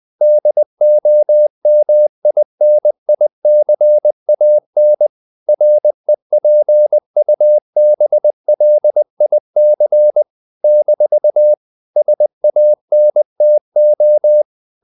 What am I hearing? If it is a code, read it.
Morse code, heard as a keyed tone, about 20 wpm, 600 Hz. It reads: DOMINICAN REPUBLIC - SANTO
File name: CW / Practice / NationalCapitals20